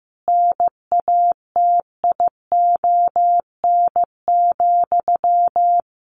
Morse code, heard as a keyed tone, about 15 wpm, 700 Hz. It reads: NATION,
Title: Sentences from Top 900 Words 15wpm